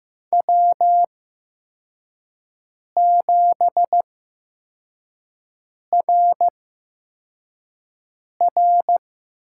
Morse code, heard as a keyed tone, about 15 wpm, spaced as for 4 wpm, 700 Hz. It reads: W7RR
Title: USA Call Signs - 4 Characters - 8x Character Spacing - 15wpm